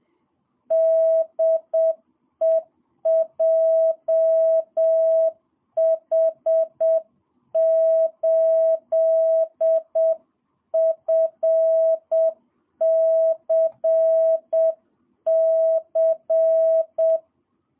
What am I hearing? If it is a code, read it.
Morse code, heard as a keyed tone, about 7 wpm, 660 Hz. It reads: DE JH8FCC
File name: jh8cbh / 2022cw